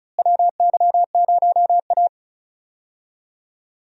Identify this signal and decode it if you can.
Morse code, heard as a keyed tone, about 36 wpm, 700 Hz. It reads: WY0A
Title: USA Call Signs - 4 Characters - Repeated 2x - 35wpm